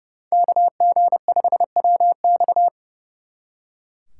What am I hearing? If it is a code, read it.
Morse code, heard as a keyed tone, about 37 wpm, 700 Hz. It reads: KG5WX
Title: Speed Racing - USA 2x2 Call Signs 30wpm